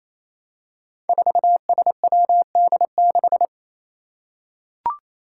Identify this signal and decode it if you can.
Morse code, heard as a keyed tone, about 28 wpm, 700 Hz. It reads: 4SWD6
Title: Five Challenging Letter-Number Combinations 28wpm